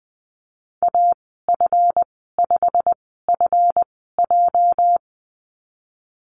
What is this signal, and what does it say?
Morse code, heard as a keyed tone, about 20 wpm, spaced as for 15 wpm, 700 Hz. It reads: AF5FJ
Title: 219 Lesson 35g - Number 0 - Call Signs 20wpm